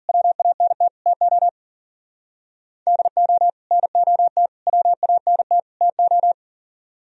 Morse code, heard as a keyed tone, about 50 wpm, 700 Hz. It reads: WANT TO DO NOT WANT TO
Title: Speed-Racing - Sets of 4 Words 40wpm